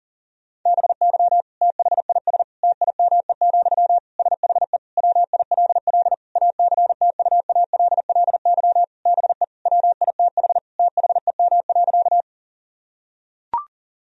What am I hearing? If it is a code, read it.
Morse code, heard as a keyed tone, about 40 wpm, 700 Hz. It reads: BY THIS TIME, SHE WILL ACTUALLY BE WITH THEM.
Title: Sentences from Top 1000 Words - Review Part 4 - 40wpm